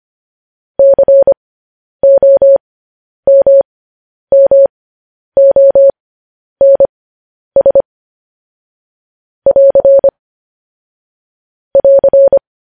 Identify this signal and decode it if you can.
Morse code, heard as a keyed tone, about 25 wpm, spaced as for 10 wpm, 550 Hz. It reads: COMMONS <AR> <AR>
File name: podcast / News headlines-1x-25x10